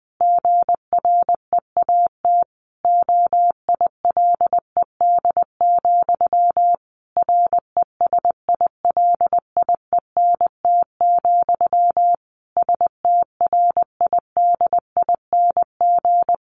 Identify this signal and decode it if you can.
Morse code, heard as a keyed tone, about 20 wpm, 700 Hz. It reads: GREAT OILED, RESILIENT, STRIDING
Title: A Sound Of Thunder 20wpm